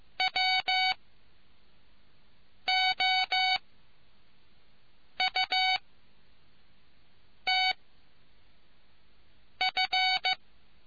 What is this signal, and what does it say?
Morse code, heard as a keyed tone, about 15 wpm, spaced as for 4 wpm, 750 Hz. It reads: WOUTF